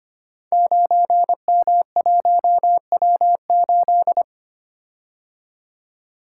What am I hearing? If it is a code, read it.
Morse code, heard as a keyed tone, about 25 wpm, 700 Hz. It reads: 9M1W8
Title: Code Groups - 5 Letter:Numbers - 25wpm